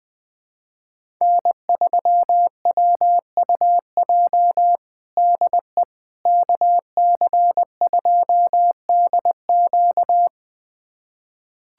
Morse code, heard as a keyed tone, about 20 wpm, 700 Hz. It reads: N3WUJ DE KC2DQ
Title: USA Callsign DE Callsign - 5 Characters - 20wpm